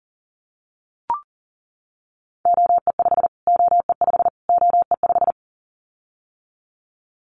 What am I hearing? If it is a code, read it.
Morse code, heard as a keyed tone, about 40 wpm, 700 Hz. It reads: OE5 OE5 OE5
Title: Callsign Prefixes for Top 25 DXCC - Repeated 3x - 40wpm